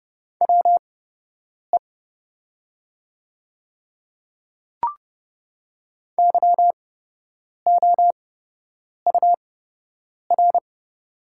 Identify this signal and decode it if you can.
Morse code, heard as a keyed tone, about 30 wpm, spaced as for 8 wpm, 700 Hz. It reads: WE YOUR
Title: 8x Character Spacing - Top 300 Words - 30wpm